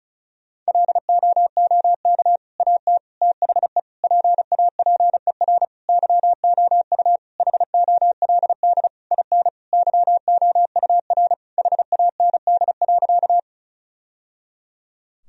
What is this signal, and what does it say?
Morse code, heard as a keyed tone, about 34 wpm, 700 Hz. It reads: LOOK AT THE PAPER YOU HOLD IN YOUR HAND.
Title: Sentences from Top 1000 Words - Review Part 2 - 35wpm